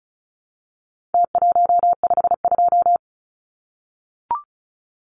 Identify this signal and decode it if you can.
Morse code, heard as a keyed tone, about 34 wpm, 700 Hz. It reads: T152
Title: Numbers - Contest Style - 4-Digit 35wpm